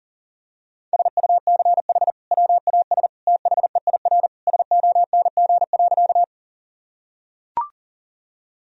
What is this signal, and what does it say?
Morse code, heard as a keyed tone, about 37 wpm, 700 Hz. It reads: SUCH WAS THEIR SONG.
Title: Sentences from Top 1000 Words - Review Part 2 - 40wpm